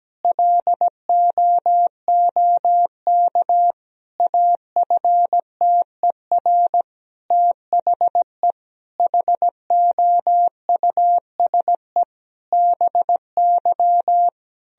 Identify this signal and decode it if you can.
Morse code, heard as a keyed tone, about 17 wpm, 700 Hz. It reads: LOOK AFTER THE HOUSE BY
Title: Sentences from Top 1000 Words - Review Part 4 - 17wpm